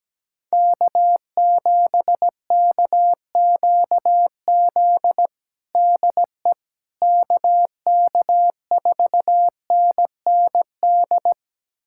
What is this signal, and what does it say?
Morse code, heard as a keyed tone, about 17 wpm, 700 Hz. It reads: K7KQZ DE KK4NND
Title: USA Callsign DE Callsign 17wpm